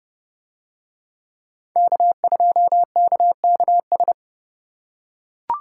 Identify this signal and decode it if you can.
Morse code, heard as a keyed tone, about 28 wpm, 700 Hz. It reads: K2KKS